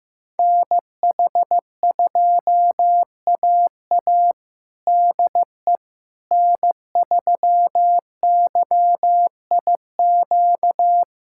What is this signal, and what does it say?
Morse code, heard as a keyed tone, about 15 wpm, 700 Hz. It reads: NH2AA DE N3YIQ
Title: USA Callsign DE Callsign - 5 Characters - 15wpm